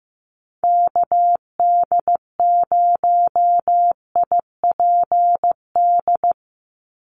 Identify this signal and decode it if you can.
Morse code, heard as a keyed tone, about 15 wpm, 700 Hz. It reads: KD0IPD
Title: USA Call Signs 15wpm